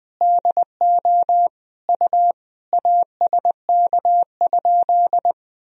Morse code, heard as a keyed tone, about 20 wpm, 700 Hz. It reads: DO U ASK?
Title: Sentences from Top 200 Words and Abbreviations 20wpm